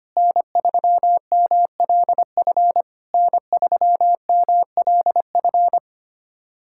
Morse code, heard as a keyed tone, about 25 wpm, 690 Hz. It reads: N3MLF N3MLF
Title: SOTA QSO Practice 25wpm